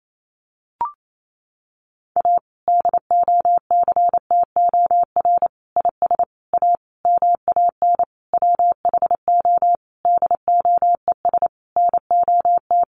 Morse code, heard as a keyed tone, about 28 wpm, 700 Hz. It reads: A DOCTOR IS A MAN WHO DOES NOT
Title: Sentences from Top 1500 Words 28wpm